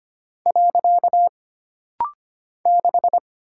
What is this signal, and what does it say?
Morse code, heard as a keyed tone, about 25 wpm, 700 Hz. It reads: . 6
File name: Single Character (Letter:Number:Punctuation:Prosign) - Rapid-Fire - 25wpm